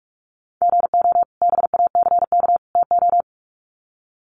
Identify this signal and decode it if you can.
Morse code, heard as a keyed tone, about 46 wpm, 700 Hz. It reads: GO BACK TO
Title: Sets of 3 Words 45wpm